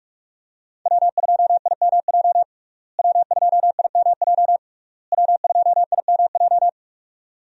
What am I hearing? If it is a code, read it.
Morse code, heard as a keyed tone, about 46 wpm, 700 Hz. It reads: W2IMJ W2IMJ W2IMJ
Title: USA Call Signs - 5 Characters - Repeated 3x - 45wpm